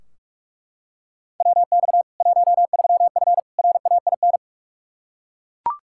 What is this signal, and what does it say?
Morse code, heard as a keyed tone, about 42 wpm, 700 Hz. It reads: WX 13F RAIN
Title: Practice QSO - Exchange 2 - Info - 45wpm